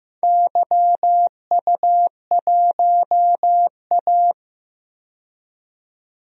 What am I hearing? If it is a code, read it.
Morse code, heard as a keyed tone, about 15 wpm, 700 Hz. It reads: YU1A
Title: DXCC-100 Call Signs - 4 Characters - 15wpm